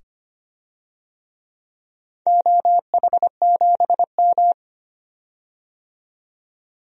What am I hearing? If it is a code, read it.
Morse code, heard as a keyed tone, about 25 wpm, 700 Hz. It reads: OH7M
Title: DXCC-100 Call Signs - 4 Characters - 25wpm